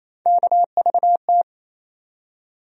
Morse code, heard as a keyed tone, about 28 wpm, 700 Hz. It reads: KVT
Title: USA Call Sign Postfixes - 3 Letters 28wpm